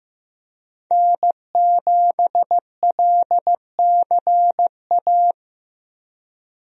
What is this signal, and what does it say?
Morse code, heard as a keyed tone, about 16 wpm, 700 Hz. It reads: N7LCA
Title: USA Call Signs - 5 Characters - Repeated 2x - 15wpm